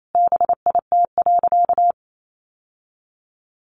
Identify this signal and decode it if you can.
Morse code, heard as a keyed tone, about 28 wpm, 690 Hz. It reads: BIT.
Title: Sentences from Top 900 Words 28wpm